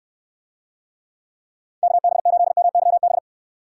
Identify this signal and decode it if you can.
Morse code, heard as a keyed tone, about 69 wpm, 700 Hz. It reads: KB8MYB